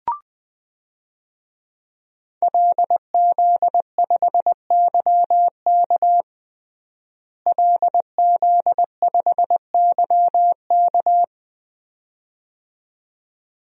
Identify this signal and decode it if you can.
Morse code, heard as a keyed tone, about 20 wpm, 700 Hz. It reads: LZ5YK LZ5YK
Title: DXCC-100 Call Signs - 5 Characters - Repeated 2x - 20wpm